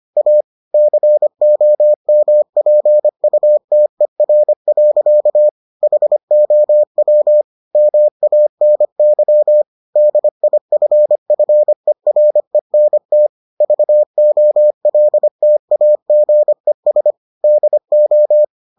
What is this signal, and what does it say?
Morse code, heard as a keyed tone, about 25 wpm, 600 Hz. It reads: A COMPUTER. HOW MANY DIFFERENT VOLTAGES DO